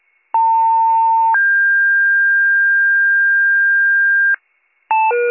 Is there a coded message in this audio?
Yes: two-tone page from 0.3 to 4.4 s, heard as a one-second tone, then a three-second tone, about 900 then 1600 Hz, with noise.